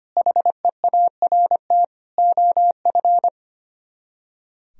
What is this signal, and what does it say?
Morse code, heard as a keyed tone, about 29 wpm, 700 Hz. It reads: HEART OF
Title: Speed-Racing - Sets of 3 Words 25wpm